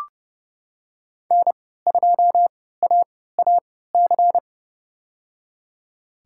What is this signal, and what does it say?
Morse code, heard as a keyed tone, about 32 wpm, spaced as for 17 wpm, 700 Hz. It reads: N2AAC
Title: USA Call Signs - 5 Characters - 3x Character Spacing - 30wpm